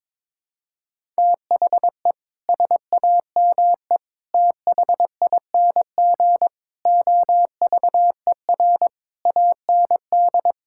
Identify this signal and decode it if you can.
Morse code, heard as a keyed tone, about 22 wpm, 700 Hz. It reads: THE SAME THING OVER AND
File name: practice / 22wpm / Sets of 6 Words - 22wpm